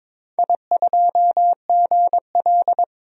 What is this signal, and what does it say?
Morse code, heard as a keyed tone, about 22 wpm, 700 Hz. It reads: I2GL
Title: DXCC-100 Call Signs - 4 Characters - 22wpm